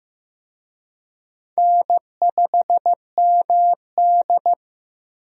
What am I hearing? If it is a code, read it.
Morse code, heard as a keyed tone, about 15 wpm, 700 Hz. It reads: N5MD
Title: USA Call Signs - 4 Characters - Repeated 2x - 15wpm